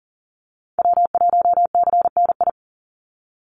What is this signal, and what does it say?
Morse code, heard as a keyed tone, about 40 wpm, 700 Hz. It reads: W1CNI
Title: USA Call Signs 40wpm